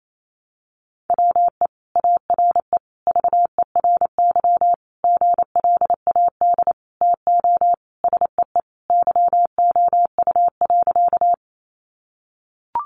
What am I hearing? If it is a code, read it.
Morse code, heard as a keyed tone, about 28 wpm, 700 Hz. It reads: WE ARE VERY GLAD TO SEE YOU.
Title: Sentences from Top 800 Words 28wpm